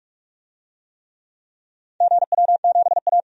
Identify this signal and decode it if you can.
Morse code, heard as a keyed tone, about 47 wpm, 700 Hz. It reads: GW7A
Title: DXCC-100 Call Signs - 4 Characters - 45wpm